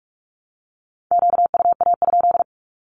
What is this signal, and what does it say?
Morse code, heard as a keyed tone, about 44 wpm, 700 Hz. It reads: QUA?
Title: Q-Codes Repeated 3x 45wpm